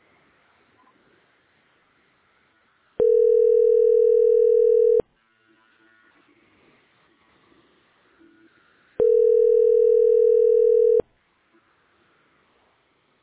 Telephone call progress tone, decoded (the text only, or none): ringback tone